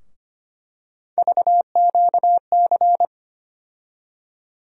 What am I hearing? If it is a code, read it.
Morse code, heard as a keyed tone, about 25 wpm, 700 Hz. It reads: VQC
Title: USA Call Sign Postfixes - 3 Letters 25wpm